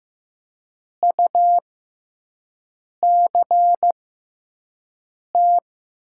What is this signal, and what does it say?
Morse code, heard as a keyed tone, about 15 wpm, spaced as for 5 wpm, 700 Hz. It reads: UCT
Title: 6x Character Spacing - Top 500 Words - 15wpm